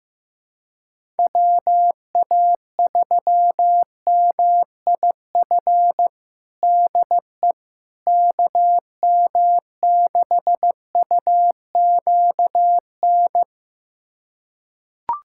Morse code, heard as a keyed tone, about 15 wpm, 700 Hz. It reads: WA3MIF DE KM6UQN
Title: USA Callsign DE Callsign - 6 Characters - 15wpm